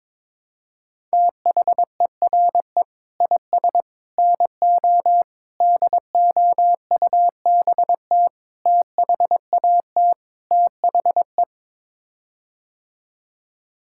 Morse code, heard as a keyed tone, about 21 wpm, 700 Hz. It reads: THERE IS NO DOUBT THAT THE
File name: practice / 22wpm / Sets of 6 Words - 22wpm